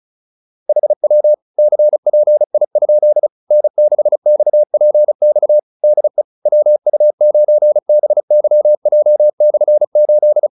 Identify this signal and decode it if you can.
Morse code, heard as a keyed tone, about 34 wpm, 600 Hz. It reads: HW CPI? N6XPX DE WU9BYJ/8
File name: QSO306 35